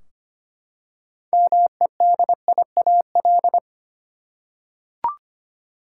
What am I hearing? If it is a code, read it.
Morse code, heard as a keyed tone, about 25 wpm, 700 Hz. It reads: MEDIAL
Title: Practice QSO - Exchange 3 - More Info - 25wpm